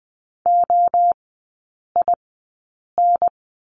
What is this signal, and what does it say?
Morse code, heard as a keyed tone, about 20 wpm, spaced as for 8 wpm, 700 Hz. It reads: OIN
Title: 035 Lesson 7e - Letter s - Three Characters 20wpm